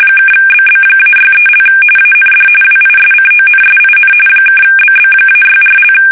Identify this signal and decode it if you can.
RTTY, about 45.45 baud. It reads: (BB-63) AND WISCONSIN (BB-64)
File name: fsk850 / battleshipnj